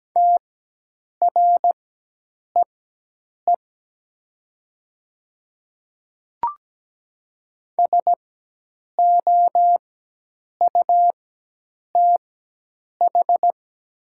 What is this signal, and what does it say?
Morse code, heard as a keyed tone, about 17 wpm, spaced as for 8 wpm, 700 Hz. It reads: TREE SOUTH
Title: 4x Character Spacing - Top 500 Words - 17wpm